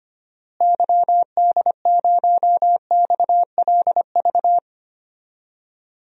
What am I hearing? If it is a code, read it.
Morse code, heard as a keyed tone, about 25 wpm, 700 Hz. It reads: YD0XLV
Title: DXCC-100 Call Signs - 6 Characters - Repeated 3x - 25wpm